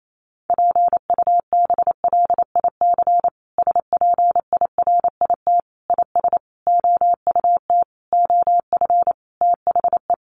Morse code, heard as a keyed tone, about 28 wpm, 700 Hz. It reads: PUBLIC SPIRIT IS OUT OF THE
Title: Sentences from Top 1400 Words 28wpm